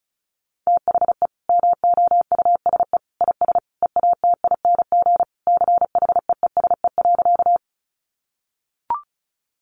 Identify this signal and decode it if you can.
Morse code, heard as a keyed tone, about 34 wpm, 700 Hz. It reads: THE MOUSE IS EATING CHEESE.
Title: Easy Phrases 35wpm